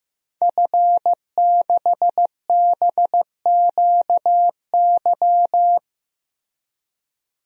Morse code, heard as a keyed tone, about 15 wpm, 700 Hz. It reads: F6BQY
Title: DXCC-100 Call Signs - 5 Characters - Repeated 3x - 15wpm